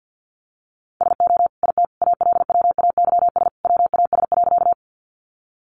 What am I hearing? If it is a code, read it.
Morse code, heard as a keyed tone, about 48 wpm, 700 Hz. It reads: SO IT ALWAYS WAS.
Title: Sentences from Top 300 Words 50wpm